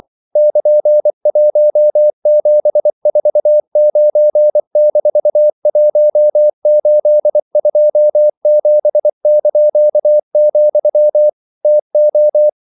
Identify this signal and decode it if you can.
Morse code, heard as a keyed tone, about 24 wpm, 600 Hz. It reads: <KN>1749-1827), TO
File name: CW / Practice / EnglishQuotations0424